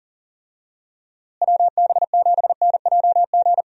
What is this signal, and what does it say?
Morse code, heard as a keyed tone, about 40 wpm, 700 Hz. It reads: WB7NJG
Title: USA Call Signs - 6 Characters - Repeated 3x - 40wpm